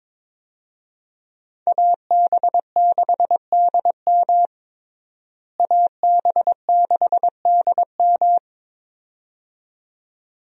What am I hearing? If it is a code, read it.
Morse code, heard as a keyed tone, about 22 wpm, 700 Hz. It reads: AB6DM AB6DM
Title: USA Call Signs - 5 Characters - Repeated 2x - 22wpm